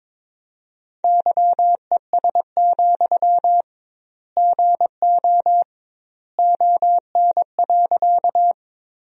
Morse code, heard as a keyed tone, about 22 wpm, 700 Hz. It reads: YES, GO ON.
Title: 2x Word-Spacing - Sentences from Top 500 Words - 22wpm